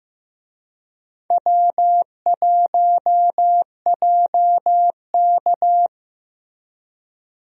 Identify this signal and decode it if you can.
Morse code, heard as a keyed tone, about 15 wpm, 700 Hz. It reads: W1JK
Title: USA Call Signs - 4 Characters - Repeated 3x - 15wpm